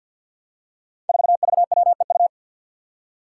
Morse code, heard as a keyed tone, about 49 wpm, 700 Hz. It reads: 4VWEU
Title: Five Challenging Letter-Number Combinations 50wpm